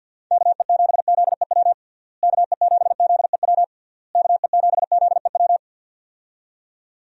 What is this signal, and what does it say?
Morse code, heard as a keyed tone, about 51 wpm, 700 Hz. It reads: KE7ZEW KE7ZEW KE7ZEW
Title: USA Call Signs - 6 Characters - Repeated 3x - 50wpm